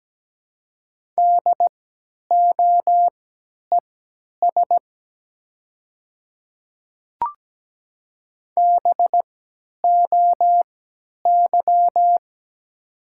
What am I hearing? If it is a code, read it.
Morse code, heard as a keyed tone, about 17 wpm, spaced as for 10 wpm, 700 Hz. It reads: DOES BOY
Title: 3x Character Spacing - Top 200 Words - 17wpm